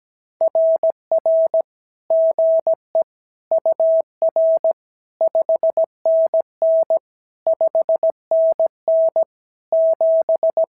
Morse code, heard as a keyed tone, about 17 wpm, 650 Hz. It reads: RR GE UR 5NN 5NN 7
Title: SOTA QSO Practice 17wpm